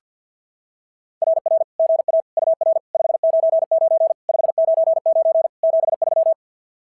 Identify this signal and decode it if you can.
Morse code, heard as a keyed tone, about 44 wpm, 660 Hz. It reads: RR GA UR 599 599 73
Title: SOTA QSO Practice 50wpm